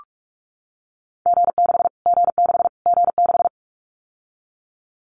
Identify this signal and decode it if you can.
Morse code, heard as a keyed tone, about 44 wpm, 700 Hz. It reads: G6 G6 G6
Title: Callsign Prefixes for Top 25 DXCC - Repeated 3x - 45wpm